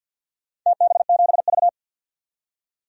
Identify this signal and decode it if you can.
Morse code, heard as a keyed tone, about 51 wpm, 700 Hz. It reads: TB7V
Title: DXCC-100 Call Signs - 4 Characters - 50wpm